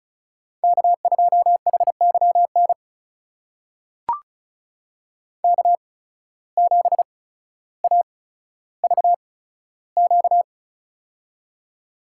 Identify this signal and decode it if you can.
Morse code, heard as a keyed tone, about 36 wpm, spaced as for 15 wpm, 700 Hz. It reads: K2HYN K7AVQ
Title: USA Call Signs - 5 Characters - 8x Character Spacing - 35wpm